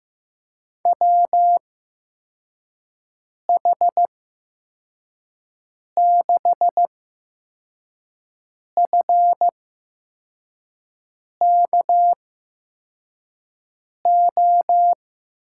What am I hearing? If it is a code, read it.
Morse code, heard as a keyed tone, about 15 wpm, spaced as for 4 wpm, 700 Hz. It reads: WH6FKO
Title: USA Call Signs - 6 Characters - 8x Character Spacing - 15wpm